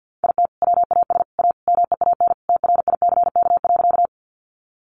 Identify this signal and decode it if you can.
Morse code, heard as a keyed tone, about 50 wpm, 700 Hz. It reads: IT WAS A MEAN TRICK.
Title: Sentences from Top 1500 Words 50wpm